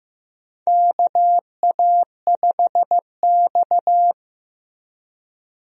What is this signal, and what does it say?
Morse code, heard as a keyed tone, about 15 wpm, 700 Hz. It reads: KA5X
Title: USA Call Signs - 4 Characters - 15wpm